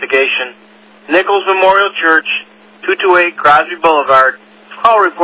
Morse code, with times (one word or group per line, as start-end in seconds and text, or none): none